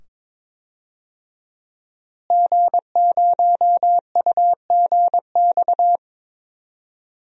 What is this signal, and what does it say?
Morse code, heard as a keyed tone, about 22 wpm, 700 Hz. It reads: G0UGX
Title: DXCC-100 Call Signs - 5 Characters - Repeated 2x - 22wpm